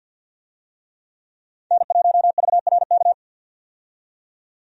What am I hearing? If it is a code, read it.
Morse code, heard as a keyed tone, about 52 wpm, 700 Hz. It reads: N1VRK
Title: USA Call Signs - 5 Characters - 50wpm